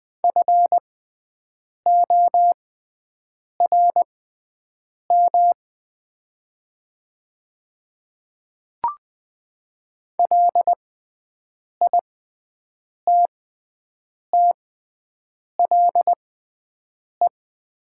Morse code, heard as a keyed tone, about 20 wpm, spaced as for 7 wpm, 700 Hz. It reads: FORM LITTLE
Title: 6x Character Spacing - Top 500 Words - 20wpm